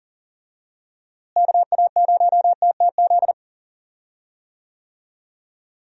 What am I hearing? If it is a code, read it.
Morse code, heard as a keyed tone, about 40 wpm, 700 Hz. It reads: KA0TTZ
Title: USA Call Signs - 6 Characters - Repeated 2x - 40wpm